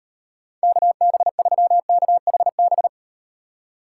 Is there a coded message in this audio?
Yes: Morse code from 0.6 to 2.9 s, heard as a keyed tone, about 38 wpm, 700 Hz. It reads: KB3KHB